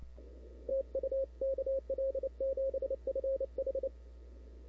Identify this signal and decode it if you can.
Morse code, heard as a keyed tone, about 29 wpm, 540 Hz. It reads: TU KL7FH
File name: CW / 2005-k5tr arrl cw dx -00:00